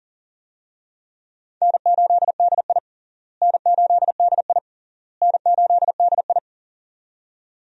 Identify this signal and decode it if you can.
Morse code, heard as a keyed tone, about 37 wpm, 700 Hz. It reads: N8DI N8DI N8DI
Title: USA Call Signs - 4 Characters - Repeated 3x - 40wpm